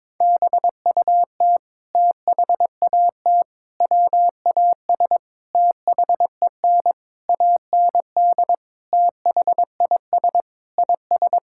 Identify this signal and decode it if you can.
Morse code, heard as a keyed tone, about 22 wpm, 700 Hz. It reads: BUT THAT WAS THEN AND THIS IS
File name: practice / 22wpm / Sentences with 8-Word Phrases - 22wpm